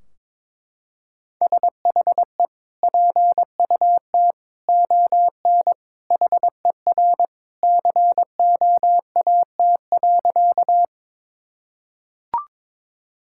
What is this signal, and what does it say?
Morse code, heard as a keyed tone, about 22 wpm, 700 Hz. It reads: SHE PUT ON HER COAT.
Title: Sentences from Top 1600 Words 22wpm